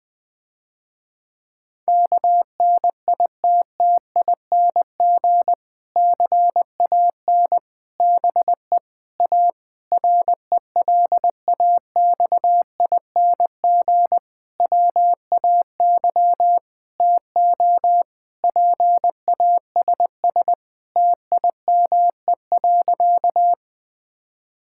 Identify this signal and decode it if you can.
Morse code, heard as a keyed tone, about 20 wpm, 700 Hz. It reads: KNITTING CAN BE A RELAXING WAY TO PASS TIME.
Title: Difficult Sentences with Hints - 20wpm